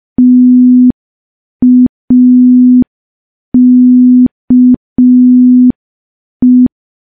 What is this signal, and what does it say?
Morse code, heard as a keyed tone, about 5 wpm, 250 Hz. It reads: TAKE
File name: morse / morse-training-05WPM